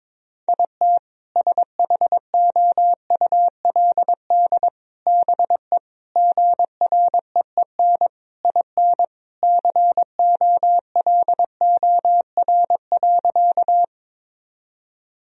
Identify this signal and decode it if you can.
Morse code, heard as a keyed tone, about 22 wpm, 700 Hz. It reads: IT SHOULD BE GREEN IN COLOR.
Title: Sentences from Top 1000 Words - Review Part 2 - 22wpm